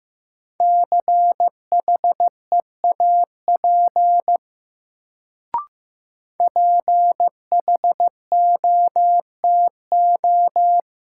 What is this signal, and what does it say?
Morse code, heard as a keyed tone, about 15 wpm, 700 Hz. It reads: CHEAP PHOTO
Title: Sentences from Top 1200 Words - New Word Drill - 15wpm